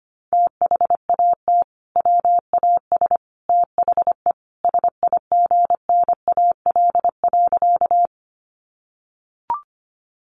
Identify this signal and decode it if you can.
Morse code, heard as a keyed tone, about 25 wpm, 700 Hz. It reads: THAT WAS THE SIGNAL.
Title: Sentences from Top 1400 Words 25wpm